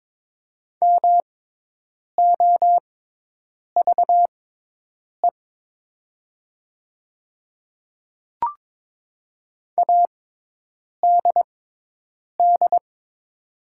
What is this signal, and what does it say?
Morse code, heard as a keyed tone, about 22 wpm, spaced as for 8 wpm, 700 Hz. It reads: MOVE ADD
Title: 6x Character Spacing - Top 200 Words - 22wpm